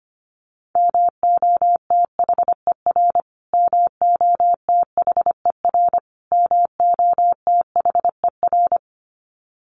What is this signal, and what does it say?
Morse code, heard as a keyed tone, about 25 wpm, 700 Hz. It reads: MOTHER MOTHER MOTHER
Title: Top 500 Words Repeated 25wpm